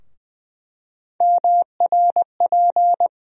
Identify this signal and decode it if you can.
Morse code, heard as a keyed tone, about 20 wpm, 700 Hz. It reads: MRP
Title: USA Call Sign Postfixes - 3 Letters 20wpm